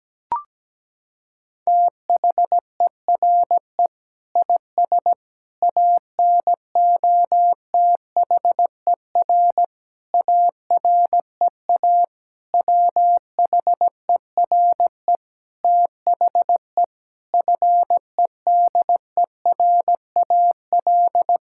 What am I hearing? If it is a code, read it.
Morse code, heard as a keyed tone, about 17 wpm, 700 Hz. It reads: THERE IS ANOTHER AREA WHERE THE FEDERAL
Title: Sentences from Top 1000 Words - Review Part 5 - 17wpm